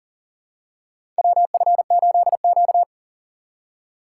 Morse code, heard as a keyed tone, about 40 wpm, 700 Hz. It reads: WF8Q
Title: USA Call Signs - 4 Characters - 4x Character Spacing - 40wpm